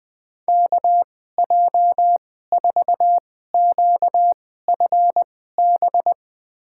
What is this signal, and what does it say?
Morse code, heard as a keyed tone, about 21 wpm, spaced as for 15 wpm, 700 Hz. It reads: KJ4QFB
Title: USA Call Signs - 2x Character Spacing - 20wpm